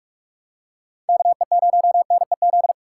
Morse code, heard as a keyed tone, about 45 wpm, 700 Hz. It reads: KE0NEZ